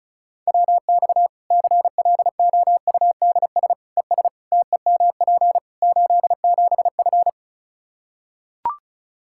Morse code, heard as a keyed tone, about 36 wpm, 700 Hz. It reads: WX CLOUDS ES TEMP 87F
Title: Practice QSO - Exchange 2 - Info - 35wpm